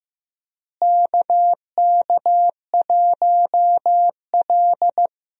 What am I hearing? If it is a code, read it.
Morse code, heard as a keyed tone, about 15 wpm, 700 Hz. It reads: KK1L
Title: USA Call Signs - 4 Characters - Repeated 2x - 15wpm